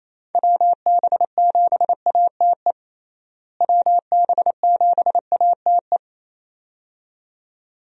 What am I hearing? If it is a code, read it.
Morse code, heard as a keyed tone, about 28 wpm, 700 Hz. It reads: WB7ATE WB7ATE
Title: USA Call Signs - 6 Characters - Repeated 2x - 28wpm